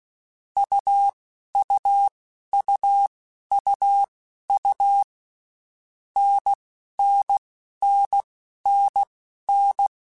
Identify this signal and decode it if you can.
Morse code, heard as a keyed tone, about 16 wpm, spaced as for 11 wpm, 800 Hz. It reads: UUUUU NNNNN
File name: Basic Letter Groups By7Groups / AWUNDG / AWUNDG@10